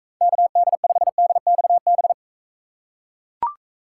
Morse code, heard as a keyed tone, about 38 wpm, 700 Hz. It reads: KD5DXB